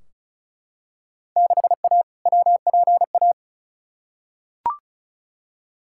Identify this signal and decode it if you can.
Morse code, heard as a keyed tone, about 36 wpm, 700 Hz. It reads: 6A WPA
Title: Field Day Exchanges Encore - 35wpm